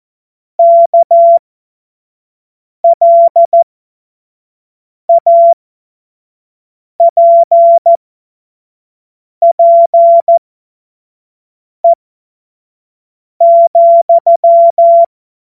Morse code, heard as a keyed tone, about 14 wpm, spaced as for 5 wpm, 680 Hz. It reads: KLAPPE,
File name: Morsekurs / Witze / Witz 16. Februar 2021, 25 F70